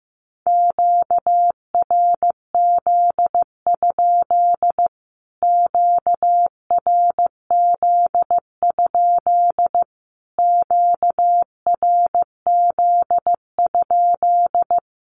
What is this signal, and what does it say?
Morse code, heard as a keyed tone, about 15 wpm, 700 Hz. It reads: QRZ? QRZ? QRZ?
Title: Q-Codes Repeated 3x 15wpm